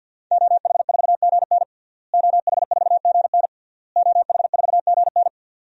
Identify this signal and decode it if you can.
Morse code, heard as a keyed tone, about 52 wpm, 700 Hz. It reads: OH4GN OH4GN OH4GN
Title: DXCC-100 Call Signs - 5 Characters - Repeated 3x - 50wpm